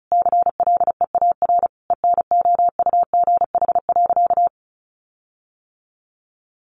Morse code, heard as a keyed tone, about 34 wpm, 700 Hz. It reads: CLEAR ENOUGH.
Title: Sentences from Top 1500 Words 35wpm